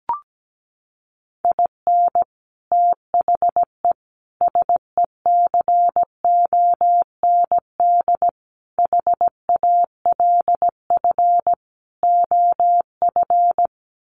Morse code, heard as a keyed tone, about 17 wpm, 700 Hz. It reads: IN THE SECOND HALF OF
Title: Sets of 5 Words 17wpm